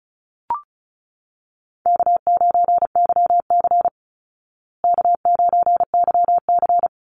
Morse code, heard as a keyed tone, about 34 wpm, 700 Hz. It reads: K9YC K9YC
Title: Easy Call Signs Repeated 3x - 35wpm